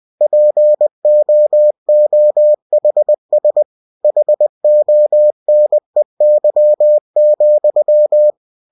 Morse code, heard as a keyed tone, about 20 wpm, 600 Hz. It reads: POOHS HONEY,
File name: cw-books / winnie-the-pooh / winnie-20wpm-page-0039